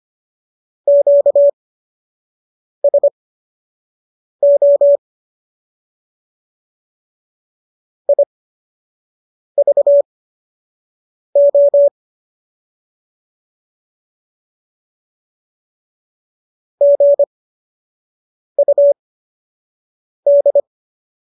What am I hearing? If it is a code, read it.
Morse code, heard as a keyed tone, about 25 wpm, spaced as for 6 wpm, 570 Hz. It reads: QSO IVO GUD
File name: qso012 06